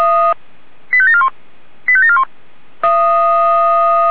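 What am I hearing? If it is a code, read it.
Five-tone selective call (ZVEI), heard as quick runs of five tones, tones of about 70 ms. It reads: 85741 85741
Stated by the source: ZVEI five-tone selective call 85722 received via SDR.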